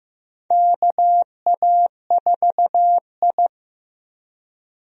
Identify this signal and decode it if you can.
Morse code, heard as a keyed tone, about 16 wpm, 700 Hz. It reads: KA4I
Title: USA Call Signs - 4 Characters - Repeated 3x - 15wpm